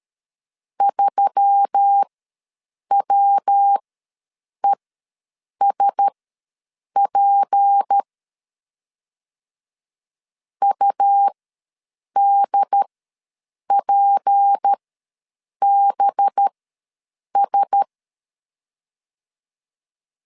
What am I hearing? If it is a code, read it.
Morse code, heard as a keyed tone, about 13 wpm, spaced as for 7 wpm, 800 Hz. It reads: 3WESP UDPBS
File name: LEZIONI / 6 PKX0 / 6 P ws 12wpm 3117